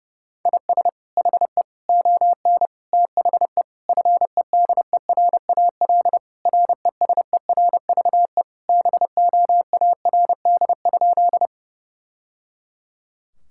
Morse code, heard as a keyed tone, about 30 wpm, 700 Hz. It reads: IS HE ON THE FEDERAL RESERVE BOARD?
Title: Sentences from Top 1000 Words - Review Part 6 - 30wpm